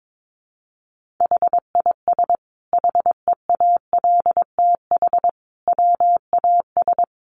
Morse code, heard as a keyed tone, about 22 wpm, 700 Hz. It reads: HIS HEALTH WAS
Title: Sentences from Top 1400 Words 22wpm